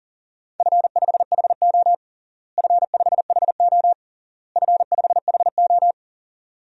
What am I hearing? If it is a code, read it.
Morse code, heard as a keyed tone, about 40 wpm, 700 Hz. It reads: F5HO F5HO F5HO
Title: DXCC-100 Call Signs - 4 Characters - Repeated 3x - 40wpm